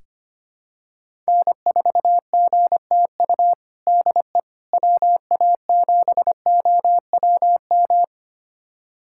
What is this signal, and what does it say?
Morse code, heard as a keyed tone, about 25 wpm, 700 Hz. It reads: N4GTU DE WA7OWM
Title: USA Callsign DE Callsign 25wpm